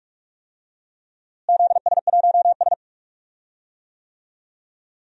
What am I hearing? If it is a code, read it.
Morse code, heard as a keyed tone, about 46 wpm, 700 Hz. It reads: ZS1S
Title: DXCC-100 Call Signs - 4 Characters - Repeated 2x - 45wpm